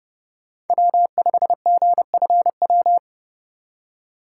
Morse code, heard as a keyed tone, about 30 wpm, 700 Hz. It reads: W5GFW
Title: USA Call Signs - 5 Characters - 2x Character Spacing - 30wpm